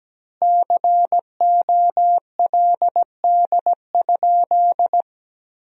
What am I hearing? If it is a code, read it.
Morse code, heard as a keyed tone, about 17 wpm, 700 Hz. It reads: COLD?
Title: 4x Word-Spacing - Sentences from Top 400 Words - 17wpm